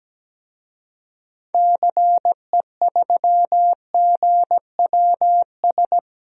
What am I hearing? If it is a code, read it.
Morse code, heard as a keyed tone, about 17 wpm, 700 Hz. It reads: CE3GWS
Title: DXCC-100 Call Signs - 6 Characters - 17wpm